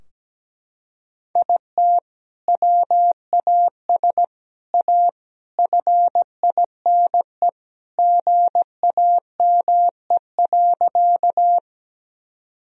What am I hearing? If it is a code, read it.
Morse code, heard as a keyed tone, about 17 wpm, 700 Hz. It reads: IT WAS A FINE GAME.
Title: Sentences from Top 1000 Words - Review Part 3 - 17wpm